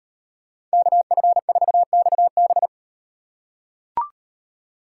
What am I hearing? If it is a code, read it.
Morse code, heard as a keyed tone, about 38 wpm, 700 Hz. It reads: KF4XB